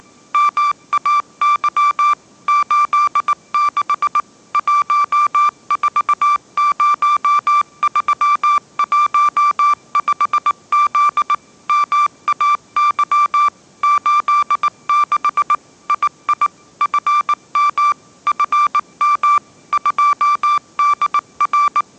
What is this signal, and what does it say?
Morse code, heard as a keyed tone, about 19 wpm, 1170 Hz. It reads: MAY 86 140315Z MAY 86 II FM FM 2DR
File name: mrx083 20wpm